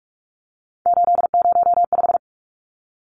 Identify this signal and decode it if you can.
Morse code, heard as a keyed tone, about 44 wpm, 700 Hz. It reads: 805
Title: Numbers - 3-Digit 45wpm